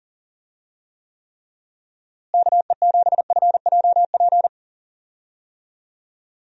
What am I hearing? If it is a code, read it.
Morse code, heard as a keyed tone, about 40 wpm, 700 Hz. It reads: KE7FJP
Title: USA Call Signs - 6 Characters - Repeated 3x - 40wpm